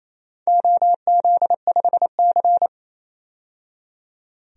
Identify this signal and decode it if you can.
Morse code, heard as a keyed tone, about 28 wpm, 700 Hz. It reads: OZ5C
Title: DXCC-100 Call Signs - 4 Characters - 28wpm